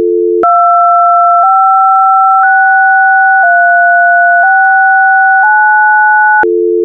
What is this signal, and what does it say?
DTMF, heard as telephone touch tones, tones of about 1010 ms. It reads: 256369